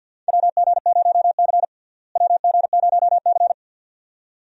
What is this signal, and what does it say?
Morse code, heard as a keyed tone, about 46 wpm, 700 Hz. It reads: WG0C WG0C
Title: USA Call Signs - 4 Characters - Repeated 2x - 50wpm